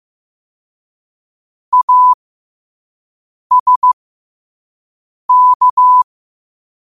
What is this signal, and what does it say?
Morse code, heard as a keyed tone, about 15 wpm, spaced as for 5 wpm, 1000 Hz. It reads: ASK
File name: qotdfiles / 5wpm